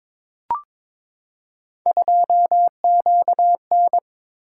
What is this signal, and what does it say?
Morse code, heard as a keyed tone, about 23 wpm, 700 Hz. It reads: 2QN
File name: practice / 22wpm / Three Letter-Number Combinations - 22wpm